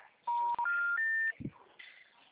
Special information tone.
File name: out- -2019- - - .185